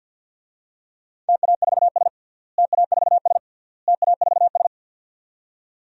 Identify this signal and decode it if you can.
Morse code, heard as a keyed tone, about 52 wpm, 700 Hz. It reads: TA4S TA4S TA4S
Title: DXCC-100 Call Signs - 4 Characters - Repeated 3x - 50wpm